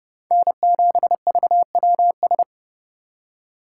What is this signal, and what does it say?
Morse code, heard as a keyed tone, about 29 wpm, 700 Hz. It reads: N7VWS
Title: USA Call Signs - 5 Characters - 8x Character Spacing - 30wpm